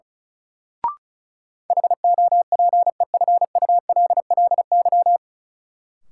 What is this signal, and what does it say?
Morse code, heard as a keyed tone, about 36 wpm, 700 Hz. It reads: HOPEFULLY
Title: Sentences from Top 1500 Words - New Word Drill Alt - 35wpm